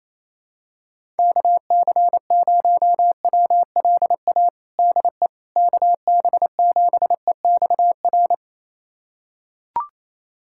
Morse code, heard as a keyed tone, about 28 wpm, 700 Hz. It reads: KC0WLA DE KB7EXR
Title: USA Callsign DE Callsign 28wpm